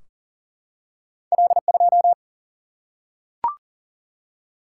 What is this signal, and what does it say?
Morse code, heard as a keyed tone, about 40 wpm, 700 Hz. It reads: L2
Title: Single-Letter Single-Number 40wpm